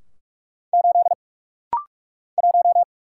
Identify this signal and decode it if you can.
Morse code, heard as a keyed tone, about 44 wpm, 690 Hz. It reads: 8 1
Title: Single Letter-Number - Rapid-Fire 45wpm